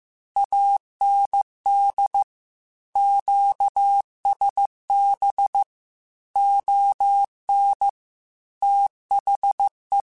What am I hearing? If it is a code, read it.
Morse code, heard as a keyed tone, about 15 wpm, 800 Hz. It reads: AND QSB ON THE
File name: QSOs / QSOs-mp3s / WD0BGP 1@10